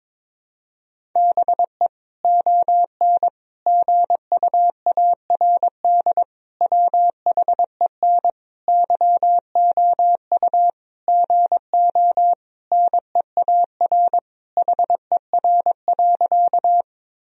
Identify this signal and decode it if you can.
Morse code, heard as a keyed tone, about 22 wpm, 700 Hz. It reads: BE ON GUARD WHEN YOU GO NEAR HER.
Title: Sentences from Top 1000 Words - Review Part 4 - 22wpm